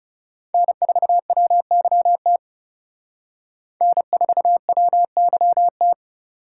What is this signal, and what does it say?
Morse code, heard as a keyed tone, about 37 wpm, 700 Hz. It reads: N4WYT N4WYT
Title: Speed Racing - USA 1x3 Call Signs 30wpm